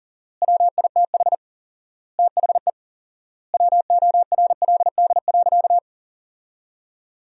Morse code, heard as a keyed tone, about 37 wpm, 700 Hz. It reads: WITH THE WORLD.
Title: 4x Word-Spacing - Sentences from Top 200 Words - 40wpm